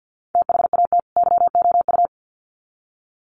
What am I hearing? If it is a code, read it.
Morse code, heard as a keyed tone, about 48 wpm, 700 Hz. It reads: THAT YOU
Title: Sets of 2 Words 50wpm